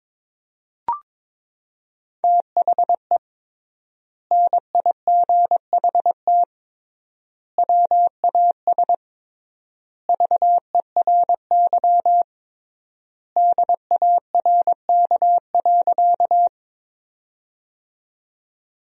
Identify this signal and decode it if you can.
Morse code, heard as a keyed tone, about 22 wpm, 700 Hz. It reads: THE NIGHT WAS VERY DARK.
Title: 3x Word-Spacing - Sentences from Top 500 Words - 22wpm